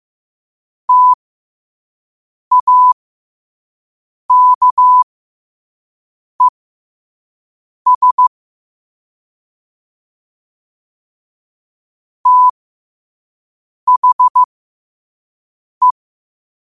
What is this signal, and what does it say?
Morse code, heard as a keyed tone, about 15 wpm, spaced as for 5 wpm, 1000 Hz. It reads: TAKES THE
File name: qotdfiles / 5wpm